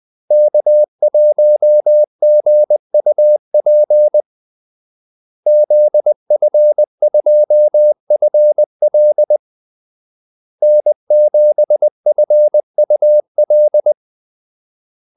Morse code, heard as a keyed tone, about 20 wpm, 600 Hz. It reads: K1GUP ZF2FL N7FUL
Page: Kaldesignaler 20 wpm